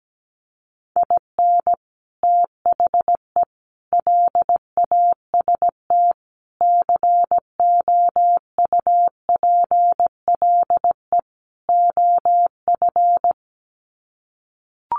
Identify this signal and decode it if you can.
Morse code, heard as a keyed tone, about 17 wpm, 700 Hz. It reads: IN THE LAST COUPLE OF
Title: Sets of 5 Words 17wpm